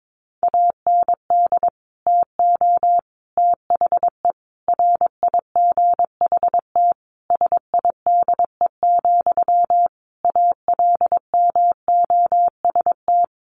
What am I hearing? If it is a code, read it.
Morse code, heard as a keyed tone, about 22 wpm, 700 Hz. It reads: AND TO THE RIGHT SIDE, ALMOST
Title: Sentences from Top 1300 Words 22wpm